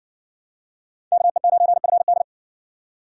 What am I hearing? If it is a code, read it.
Morse code, heard as a keyed tone, about 66 wpm, 700 Hz. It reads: KE9FD